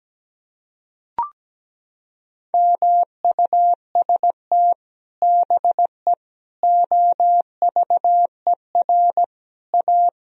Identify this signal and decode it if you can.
Morse code, heard as a keyed tone, about 17 wpm, 700 Hz. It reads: MUST BE OVER A
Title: Sentences from Top 1000 Words - Review Part 2 - 17wpm